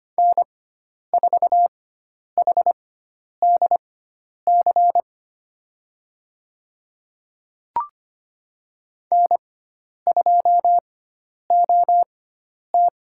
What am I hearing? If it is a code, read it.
Morse code, heard as a keyed tone, about 25 wpm, spaced as for 10 wpm, 700 Hz. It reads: N4HDC N2OT
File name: Lesson 16g - Letter c - Call Signs 25wpm